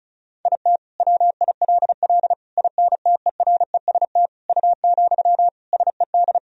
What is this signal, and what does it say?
Morse code, heard as a keyed tone, about 36 wpm, 700 Hz. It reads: IT WILL INTEREST U, SED
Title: Sentences from Top 400 Words and Abbreviations 35wpm